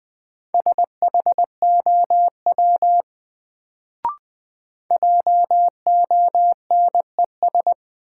Morse code, heard as a keyed tone, about 20 wpm, 700 Hz. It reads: SHOW JONES
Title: Sets of 5 Words - New Word Drill - 20wpm